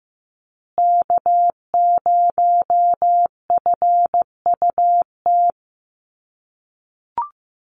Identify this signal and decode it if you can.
Morse code, heard as a keyed tone, about 15 wpm, 700 Hz. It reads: K0FUT
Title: USA Call Signs Repeated 3x 15wpm